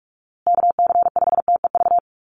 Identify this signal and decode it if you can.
Morse code, heard as a keyed tone, about 44 wpm, 700 Hz. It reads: KC5TEV
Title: USA Call Signs Repeated 2x 45wpm